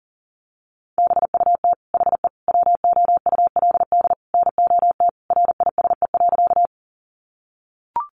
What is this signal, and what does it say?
Morse code, heard as a keyed tone, about 40 wpm, 700 Hz. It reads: BUT HE WOULD NOT RISE.
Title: Sentences from Top 900 Words 40wpm